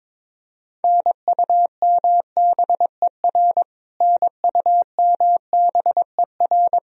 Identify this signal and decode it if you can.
Morse code, heard as a keyed tone, about 22 wpm, 700 Hz. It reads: NUMBER NUMBER
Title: Top 100 Words Repeated 22wpm